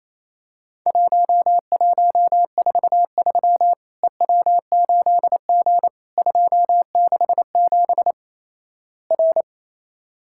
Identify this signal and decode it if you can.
Morse code, heard as a keyed tone, about 28 wpm, 700 Hz. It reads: 1143 EW8G 267 R
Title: Worked All Europe DX Contest - QTC Exchange - 28wpm